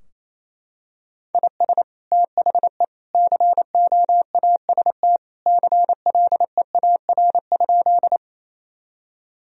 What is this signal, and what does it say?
Morse code, heard as a keyed tone, about 28 wpm, 700 Hz. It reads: IS THE COAST CLEAR?
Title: Sentences from Top 1000 Words - Review Part 4 - 28wpm